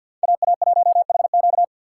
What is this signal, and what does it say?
Morse code, heard as a keyed tone, about 53 wpm, 700 Hz. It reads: AA1HQ
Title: USA Call Signs - 5 Characters - 3x Character Spacing - 50wpm